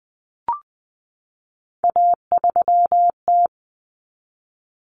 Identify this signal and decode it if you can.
Morse code, heard as a keyed tone, about 20 wpm, 700 Hz. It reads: A3T
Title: Common Cut Numbers - 3-Digit 20wpm